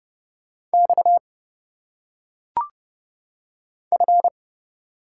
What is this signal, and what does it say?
Morse code, heard as a keyed tone, about 30 wpm, 700 Hz. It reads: X F
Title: Single Letters - Phonetic Pronunciation 30wpm